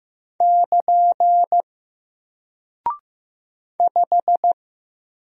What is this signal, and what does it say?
Morse code, heard as a keyed tone, about 15 wpm, 700 Hz. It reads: <KN> 5
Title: Single Character (Letter:Number:Punctuation:Prosign) Alt 15wpm